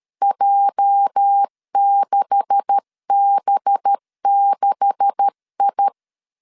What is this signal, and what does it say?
Morse code, heard as a keyed tone, about 13 wpm, 800 Hz. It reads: J6B6I
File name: LEZIONI / 9 286 12wpm 2305